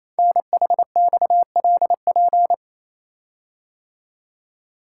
Morse code, heard as a keyed tone, about 29 wpm, 700 Hz. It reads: NHXLP
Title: Code Groups - 5 Letter:Numbers - 28wpm